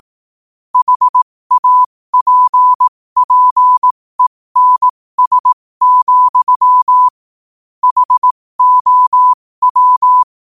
Morse code, heard as a keyed tone, about 18 wpm, spaced as for 15 wpm, 1000 Hz. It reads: HAPPENS, HOW
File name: qotdfiles / 15wpm